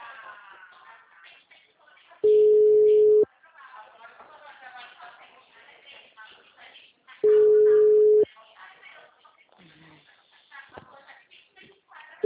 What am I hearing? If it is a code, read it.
Ringback tone (Europe, 425 Hz), with speech.